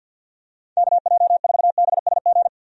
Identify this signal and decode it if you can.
Morse code, heard as a keyed tone, about 50 wpm, 700 Hz. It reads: KJ4BSG
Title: USA Call Signs - 6 Characters - 50wpm